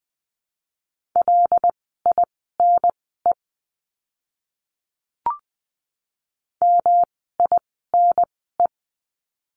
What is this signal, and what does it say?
Morse code, heard as a keyed tone, about 20 wpm, spaced as for 14 wpm, 700 Hz. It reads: LINE MINE
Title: 108 Lesson 18f - Letter m - Words 20wpm